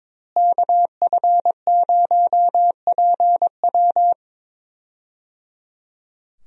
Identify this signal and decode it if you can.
Morse code, heard as a keyed tone, about 22 wpm, 700 Hz. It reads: KF0PW
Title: USA Call Signs - 5 Characters - Repeated 3x - 22wpm